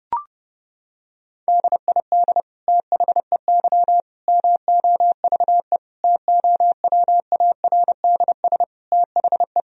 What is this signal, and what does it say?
Morse code, heard as a keyed tone, about 32 wpm, 700 Hz. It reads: DID THEY MOVE TOWARDS THE
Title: Sentences from Top 1000 Words - Review Part 2 - 30wpm